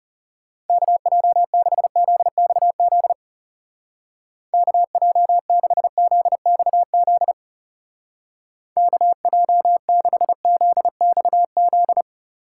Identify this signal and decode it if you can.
Morse code, heard as a keyed tone, about 36 wpm, 700 Hz. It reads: KJ6ZXZ KJ6ZXZ KJ6ZXZ
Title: Speed Racing - USA 2x3 Call Signs 30wpm